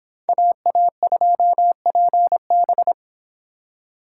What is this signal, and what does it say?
Morse code, heard as a keyed tone, about 31 wpm, 700 Hz. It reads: AA2PB